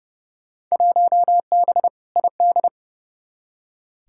Morse code, heard as a keyed tone, about 31 wpm, 700 Hz. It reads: 1B ID
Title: Field Day Exchanges - 30wpm